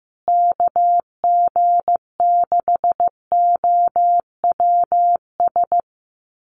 Morse code, heard as a keyed tone, about 15 wpm, 700 Hz. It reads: KG6OWS
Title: USA Call Signs Repeated 3x 15wpm